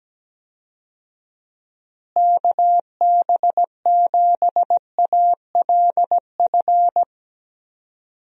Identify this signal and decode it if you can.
Morse code, heard as a keyed tone, about 17 wpm, 700 Hz. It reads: KB7ALF
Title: USA Call Signs - 6 Characters - Repeated 3x - 17wpm